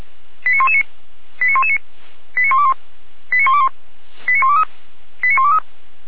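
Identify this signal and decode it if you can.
Five-tone selective call (ZVEI), heard as quick runs of five tones, tones of about 70 ms. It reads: 89119 89119 89121 89121 89123 89123
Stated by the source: ZVEI five-tone selective call received on SDR, Feuerwehr Orlamünde.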